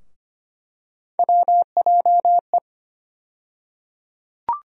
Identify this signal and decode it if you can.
Morse code, heard as a keyed tone, about 25 wpm, 700 Hz. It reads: WJE
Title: USA Call Sign Postfixes - 3 Letters 25wpm